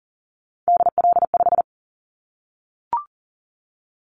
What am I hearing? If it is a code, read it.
Morse code, heard as a keyed tone, about 40 wpm, 700 Hz. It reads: DL5
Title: Callsign Prefixes for Top 25 DXCC - Repeated 3x - 40wpm